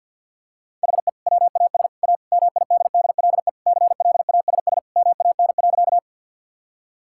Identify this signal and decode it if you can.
Morse code, heard as a keyed tone, about 50 wpm, 700 Hz. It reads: HE WAS A MIDDLE CLASS MAN.
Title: Sentences from Top 1000 Words - Review Part 6 - 50wpm